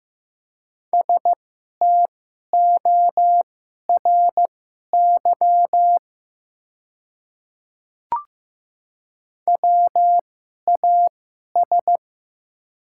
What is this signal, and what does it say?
Morse code, heard as a keyed tone, about 15 wpm, spaced as for 11 wpm, 700 Hz. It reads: STORY WAS
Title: 2x Character Spacing - Top 300 Words - 15wpm